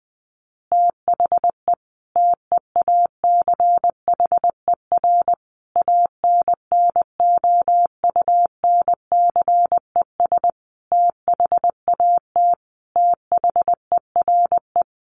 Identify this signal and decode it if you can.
Morse code, heard as a keyed tone, about 20 wpm, 700 Hz. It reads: THE TEACHER ANNOUNCES THAT THERE
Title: Additional Easy Phrases 20wpm